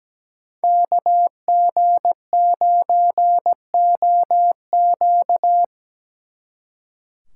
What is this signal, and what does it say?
Morse code, heard as a keyed tone, about 20 wpm, 700 Hz. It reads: KG9OQ
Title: Speed Racing - USA 2x2 Call Signs 17wpm